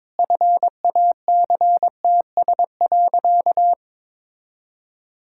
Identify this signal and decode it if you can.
Morse code, heard as a keyed tone, about 22 wpm, 700 Hz. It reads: FACTS.
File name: practice / 22wpm / Sentences with 8-Word Phrases - 22wpm